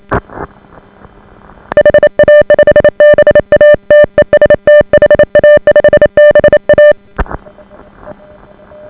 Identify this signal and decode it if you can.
Morse code, heard as a keyed tone, about 28 wpm, 590 Hz. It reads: HA5BA TEST HA5BA